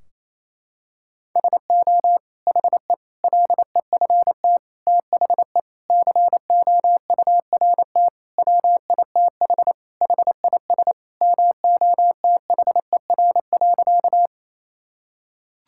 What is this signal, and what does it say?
Morse code, heard as a keyed tone, about 28 wpm, 700 Hz. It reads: SO HE LEFT THE COURT WITH HIS MOTHER.
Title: Sentences from Top 1600 Words 28wpm